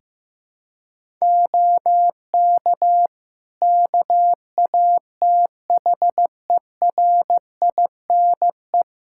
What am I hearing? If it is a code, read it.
Morse code, heard as a keyed tone, about 15 wpm, 700 Hz. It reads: OK KATHERINE
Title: K1USN SST QSO Practice 15wpm